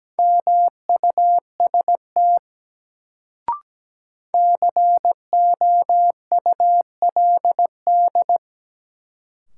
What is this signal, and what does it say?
Morse code, heard as a keyed tone, about 17 wpm, 700 Hz. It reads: MUST COULD
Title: Sets of 2 Words - New Word Drill - 17wpm